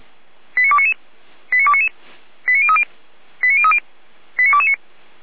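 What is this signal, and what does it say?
Five-tone selective call (ZVEI), heard as quick runs of five tones, tones of about 70 ms. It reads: 89299 89299 89030 89030 89229
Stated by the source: ZVEI five-tone selective call received by SDR, from the feuerwehr-orlamuende fire brigade audio server.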